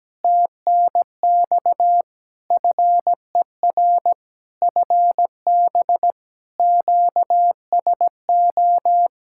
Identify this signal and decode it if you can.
Morse code, heard as a keyed tone, about 17 wpm, 700 Hz. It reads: TNX FER FB QSO
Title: K1USN SST QSO Practice 17wpm